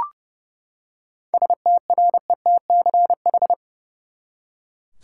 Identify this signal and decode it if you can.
Morse code, heard as a keyed tone, about 30 wpm, 700 Hz. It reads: STRETCH
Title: Seven Letter Words - 30wpm